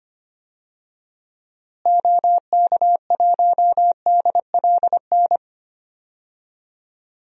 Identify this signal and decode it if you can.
Morse code, heard as a keyed tone, about 25 wpm, 700 Hz. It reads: OK1DLN
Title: DXCC-100 Call Signs - 6 Characters - Repeated 3x - 25wpm